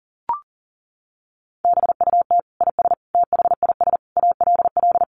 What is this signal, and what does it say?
Morse code, heard as a keyed tone, about 40 wpm, 700 Hz. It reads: BUT IS THIS ALL
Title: Sentences from Top 600 Words 40wpm